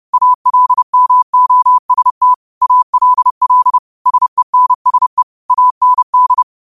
Morse code, heard as a keyed tone, about 30 wpm, 1000 Hz. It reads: ALMOST ALL SENSE AND
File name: qotdfiles / 30wpm